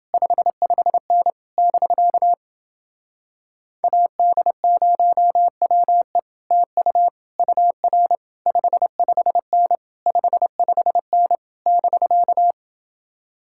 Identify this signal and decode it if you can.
Morse code, heard as a keyed tone, about 29 wpm, 700 Hz. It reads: 55N <BK> AD0WE TU UR 55N 55N <BK>